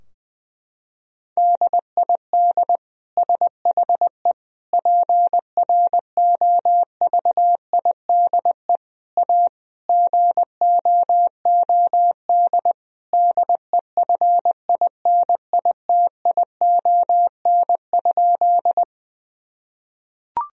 Morse code, heard as a keyed tone, about 19 wpm, 700 Hz. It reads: DID SHE PROVIDE A GOOD DEFINITION?
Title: Sentences from Top 1000 Words - Review Part 4 - 20wpm